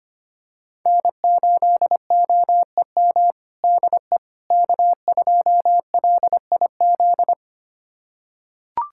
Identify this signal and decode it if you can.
Morse code, heard as a keyed tone, about 25 wpm, 700 Hz. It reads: N8OEM DE K2LIZ
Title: USA Callsign DE Callsign 25wpm